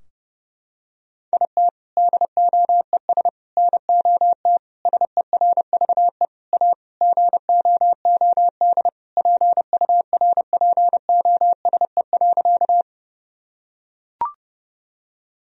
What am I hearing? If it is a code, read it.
Morse code, heard as a keyed tone, about 32 wpm, 700 Hz. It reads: IT DOES NOT SERVE A GOOD PURPOSE.
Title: Sentences from Top 1000 Words - Review Part 2 - 30wpm